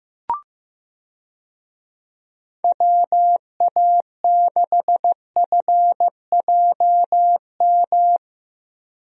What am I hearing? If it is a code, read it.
Morse code, heard as a keyed tone, about 15 wpm, 700 Hz. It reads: WA6FJM
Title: USA Call Signs - 6 Characters - Repeated 2x - 15wpm